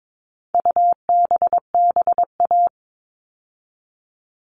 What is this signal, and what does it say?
Morse code, heard as a keyed tone, about 22 wpm, 700 Hz. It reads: UBBA
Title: All Cut Numbers - 4-Digit 22wpm